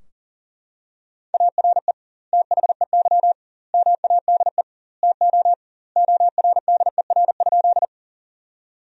Morse code, heard as a keyed tone, about 40 wpm, 700 Hz. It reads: ARE THEY MADE TO ORDER?
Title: 2x Word-Spacing - Sentences from Top 400 Words - 40wpm